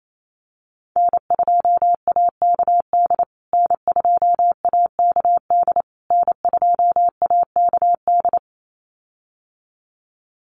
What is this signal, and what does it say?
Morse code, heard as a keyed tone, about 28 wpm, 700 Hz. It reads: N2AKD N2AKD N2AKD
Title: USA Call Signs Repeated 3x 28wpm